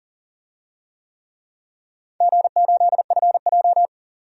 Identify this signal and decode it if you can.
Morse code, heard as a keyed tone, about 40 wpm, 700 Hz. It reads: G8FJ
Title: DXCC-100 Call Signs - 4 Characters - 40wpm